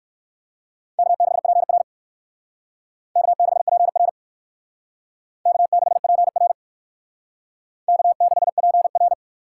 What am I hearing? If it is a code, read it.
Morse code, heard as a keyed tone, about 62 wpm, 700 Hz. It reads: K6PR K6PR K6PR K6PR